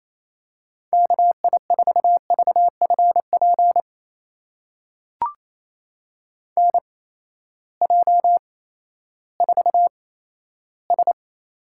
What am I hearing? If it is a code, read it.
Morse code, heard as a keyed tone, about 28 wpm, spaced as for 14 wpm, 700 Hz. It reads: KI4VFP NJ4S
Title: USA Call Signs - 8x Character Spacing - 28wpm